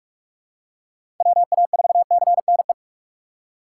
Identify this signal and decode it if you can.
Morse code, heard as a keyed tone, about 43 wpm, 700 Hz. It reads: WA4CNE
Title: USA Call Signs - 6 Characters - 3x Character Spacing - 45wpm